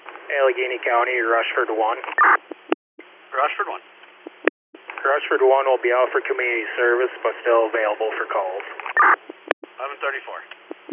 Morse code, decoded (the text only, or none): none